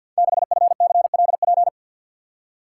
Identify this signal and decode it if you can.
Morse code, heard as a keyed tone, about 51 wpm, 700 Hz. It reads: 6FCLP
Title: Code Groups - 5 Letter:Numbers - 50wpm